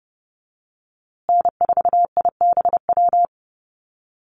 Morse code, heard as a keyed tone, about 30 wpm, 700 Hz. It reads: N4IBW
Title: USA Call Signs Repeated 3x 30wpm